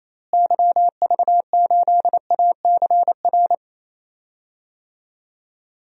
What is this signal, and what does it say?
Morse code, heard as a keyed tone, about 28 wpm, 700 Hz. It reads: YV8ACR
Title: DXCC-100 Call Signs - 6 Characters - Repeated 3x - 28wpm